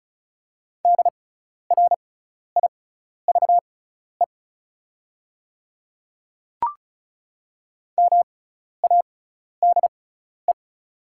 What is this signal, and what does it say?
Morse code, heard as a keyed tone, about 36 wpm, spaced as for 12 wpm, 700 Hz. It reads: DRIVE MADE
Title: 6x Character Spacing - Top 500 Words - 35wpm